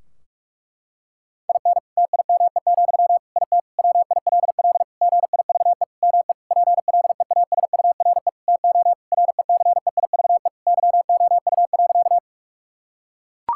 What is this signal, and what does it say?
Morse code, heard as a keyed tone, about 42 wpm, 700 Hz. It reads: IN TIME, IT WILL GIVE ME PLEASURE TO RECEIVE YOU.
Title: Sentences from Top 1600 Words 45wpm